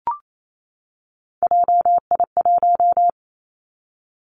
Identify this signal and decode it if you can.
Morse code, heard as a keyed tone, about 28 wpm, 700 Hz. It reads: JI1
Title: Three Challenging Letter-Number Combinations 28wpm